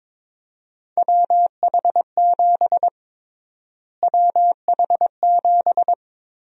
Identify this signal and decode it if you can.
Morse code, heard as a keyed tone, about 21 wpm, 700 Hz. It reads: WH7 WH7
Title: USA Call Sign Prefixes - Repeated 2x - 22wpm